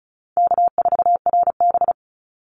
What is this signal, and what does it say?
Morse code, heard as a keyed tone, about 34 wpm, 700 Hz. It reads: K4RB
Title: Easy Call Signs Repeated 2x - 35wpm